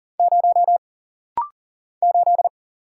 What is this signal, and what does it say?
Morse code, heard as a keyed tone, about 40 wpm, 700 Hz. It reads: 0 8
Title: Single Letter-Number Alt - Rapid-Fire 40wpm